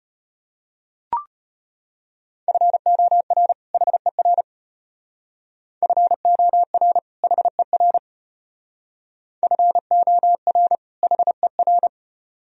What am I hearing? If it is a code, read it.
Morse code, heard as a keyed tone, about 33 wpm, 700 Hz. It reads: FOR HER FOR HER FOR HER